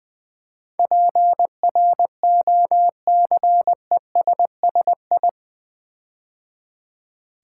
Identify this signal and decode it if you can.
Morse code, heard as a keyed tone, about 20 wpm, 700 Hz. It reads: PROCESSI
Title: Build A Word - 4 Letters At A Time - 10-Letter Words - 20wpm